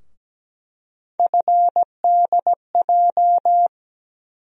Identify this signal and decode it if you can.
Morse code, heard as a keyed tone, about 17 wpm, 700 Hz. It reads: FDJ
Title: USA Call Sign Postfixes - 3 Letters 17wpm